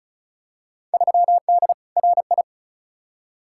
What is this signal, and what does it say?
Morse code, heard as a keyed tone, about 36 wpm, 690 Hz. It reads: 3D RI
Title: Field Day Exchanges Encore - 35wpm